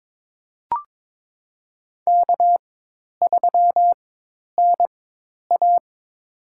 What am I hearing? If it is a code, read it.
Morse code, heard as a keyed tone, about 22 wpm, spaced as for 10 wpm, 700 Hz. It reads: K3NA
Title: USA Call Signs - 4x Character Spacing - 22wpm